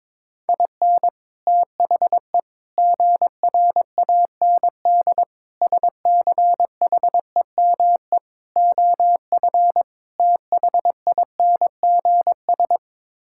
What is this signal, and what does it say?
Morse code, heard as a keyed tone, about 22 wpm, 700 Hz. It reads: IN THE GRAND SCHEME OF THINGS
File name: practice / 22wpm / Sets of 6 Words - 22wpm